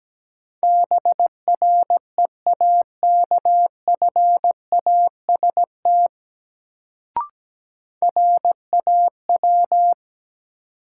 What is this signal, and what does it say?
Morse code, heard as a keyed tone, about 17 wpm, 700 Hz. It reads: BREAKFAST RAW
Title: Sentences from Top 1500 Words - New Word Drill Alt - 17wpm